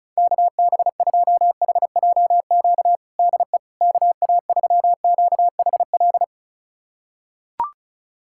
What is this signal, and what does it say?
Morse code, heard as a keyed tone, about 34 wpm, 700 Hz. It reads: KB2HJQ DE KA3QHL
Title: USA Callsign DE Callsign 35wpm